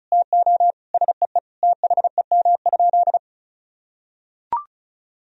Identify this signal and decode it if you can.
Morse code, heard as a keyed tone, about 34 wpm, 700 Hz. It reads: TO SEE THEM?
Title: Sentences from Top 1000 Words - Review Part 1 - 35wpm